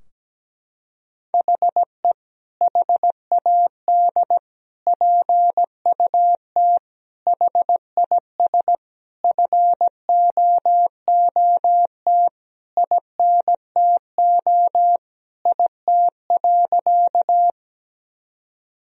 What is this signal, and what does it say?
Morse code, heard as a keyed tone, about 17 wpm, 700 Hz. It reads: HE HAD PUT HIS FOOT INTO IT.
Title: Sentences from Top 1000 Words - Review Part 3 - 17wpm